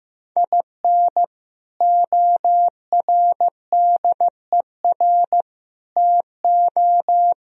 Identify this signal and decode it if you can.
Morse code, heard as a keyed tone, about 15 wpm, 700 Hz. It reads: IN ORDER TO
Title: Sentences from Top 1000 Words - Review Part 2 - 15wpm